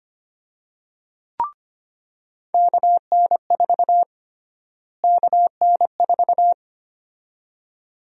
Morse code, heard as a keyed tone, about 25 wpm, 700 Hz. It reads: KN4 KN4
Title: USA Call Sign Prefixes - Repeated 2x - 25wpm